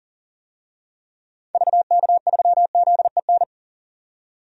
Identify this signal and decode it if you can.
Morse code, heard as a keyed tone, about 40 wpm, 700 Hz. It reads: VK3ZEN
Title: DXCC-100 Call Signs - 6 Characters - 40wpm